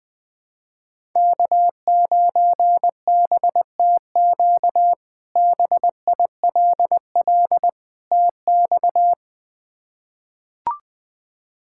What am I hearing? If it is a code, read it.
Morse code, heard as a keyed tone, about 21 wpm, 700 Hz. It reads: K9BTQ BILL TX
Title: K1USN SST QSO Practice - 20wpm